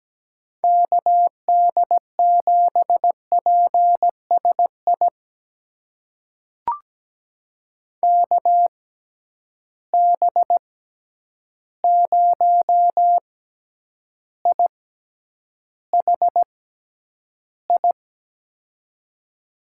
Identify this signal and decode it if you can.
Morse code, heard as a keyed tone, about 17 wpm, spaced as for 9 wpm, 700 Hz. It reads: KD7PSI KB0IHI
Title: USA Call Signs - 6 Characters - 6x Character Spacing - 17wpm